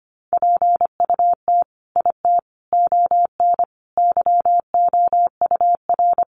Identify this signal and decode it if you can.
Morse code, heard as a keyed tone, about 25 wpm, 700 Hz. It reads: PUT IT ON YOUR
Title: Sentences from Top 1400 Words 25wpm